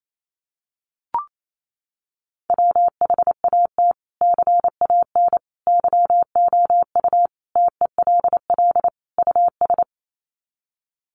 Morse code, heard as a keyed tone, about 28 wpm, 700 Hz. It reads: WHAT CAN YOU TELL US
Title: Sets of 5 Words Encore 28wpm